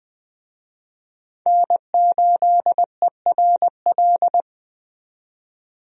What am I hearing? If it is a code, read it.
Morse code, heard as a keyed tone, about 20 wpm, 700 Hz. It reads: N8ERL
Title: USA Call Signs - 5 Characters - 20wpm